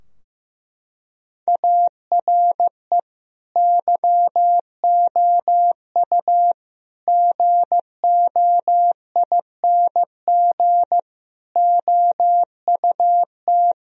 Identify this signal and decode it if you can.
Morse code, heard as a keyed tone, about 15 wpm, 700 Hz. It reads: ARE YOU GOING OUT
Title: Sentences from Top 1000 Words - Review Part 2 - 15wpm